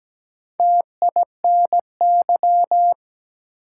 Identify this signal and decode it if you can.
Morse code, heard as a keyed tone, about 17 wpm, 700 Hz. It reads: TINY
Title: Four Letter Words - 17wpm